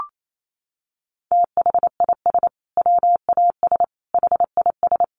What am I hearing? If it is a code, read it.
Morse code, heard as a keyed tone, about 28 wpm, 700 Hz. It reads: THIS WAS HIS
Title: Sentences from Top 1300 Words 28wpm